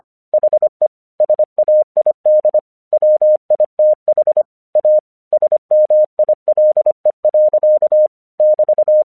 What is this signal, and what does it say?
Morse code, heard as a keyed tone, about 24 wpm, 610 Hz. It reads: HE SAID WITH A SMILE. <BT>
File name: 1 English 25wpm / AQC21